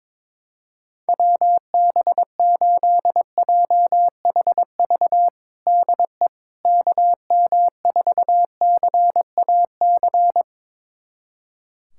Morse code, heard as a keyed tone, about 22 wpm, 700 Hz. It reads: WB8JHV DE KM4CAC
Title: USA Callsign DE Callsign 22wpm